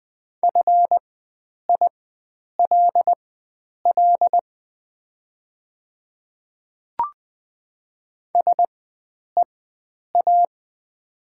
Morse code, heard as a keyed tone, about 20 wpm, spaced as for 9 wpm, 700 Hz. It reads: FILL SEA
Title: 4x Character Spacing - Top 500 Words - 20wpm